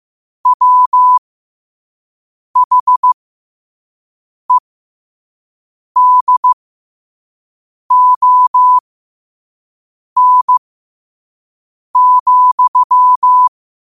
Morse code, heard as a keyed tone, about 15 wpm, spaced as for 5 wpm, 1000 Hz. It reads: WHEDON,
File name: qotdfiles / 5wpm